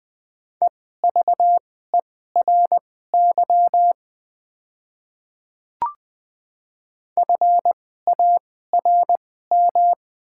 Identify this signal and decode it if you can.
Morse code, heard as a keyed tone, about 20 wpm, spaced as for 15 wpm, 690 Hz. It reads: EVERY FARM
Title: 2x Character Spacing - Top 500 Words - 20wpm